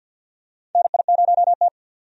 Morse code, heard as a keyed tone, about 46 wpm, 700 Hz. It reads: NI0T
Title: USA Call Signs - 4 Characters - 50wpm